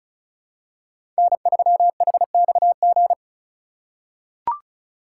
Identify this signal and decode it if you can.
Morse code, heard as a keyed tone, about 34 wpm, 700 Hz. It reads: N3HXG
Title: USA Call Signs - 5 Characters - 6x Character Spacing - 35wpm